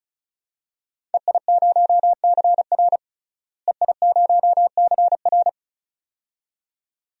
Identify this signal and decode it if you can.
Morse code, heard as a keyed tone, about 36 wpm, 700 Hz. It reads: EI0CR EI0CR
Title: DXCC-100 Call Signs - 5 Characters - Repeated 2x - 35wpm